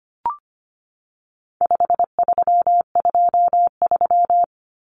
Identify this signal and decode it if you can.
Morse code, heard as a keyed tone, about 25 wpm, 700 Hz. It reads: 5323
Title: Numbers - 4-Digit 25wpm